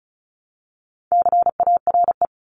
Morse code, heard as a keyed tone, about 34 wpm, 700 Hz. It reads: CARE
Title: Top 500 Words 35wpm